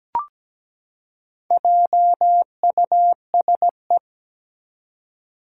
Morse code, heard as a keyed tone, about 17 wpm, 700 Hz. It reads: JUSE
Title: Four Challenging Letter-Number Combinations 17wpm